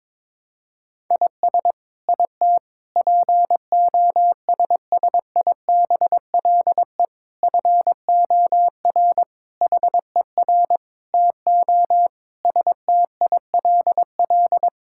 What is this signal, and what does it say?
Morse code, heard as a keyed tone, about 22 wpm, 700 Hz. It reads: IS IT POSSIBLE FOR HER TO STILL
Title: Sentences from Top 1000 Words - Review Part 6 - 22wpm